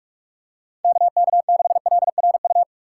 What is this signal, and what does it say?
Morse code, heard as a keyed tone, about 42 wpm, 700 Hz. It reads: KK6LRU
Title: USA Call Signs - 6 Characters - 4x Character Spacing - 45wpm